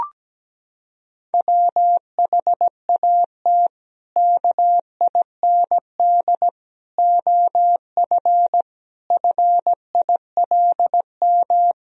Morse code, heard as a keyed tone, about 17 wpm, 700 Hz. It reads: WHAT KIND OF FILM
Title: Sentences from Top 1000 Words - Review Part 3 - 17wpm